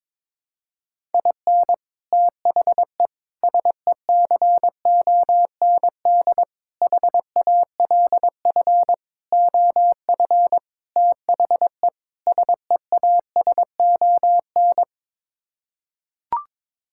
Morse code, heard as a keyed tone, about 22 wpm, 700 Hz. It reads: IN THE SECOND HALF OF THE SEASON
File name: practice / 22wpm / Sets of 7 Words - 22wpm